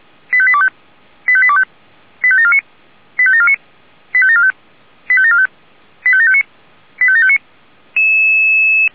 Five-tone selective call (ZVEI), heard as quick runs of five tones, tones of about 70 ms. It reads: 85825 85825 85749 85749 85745 85745 85759 85759
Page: SDR/ZVEI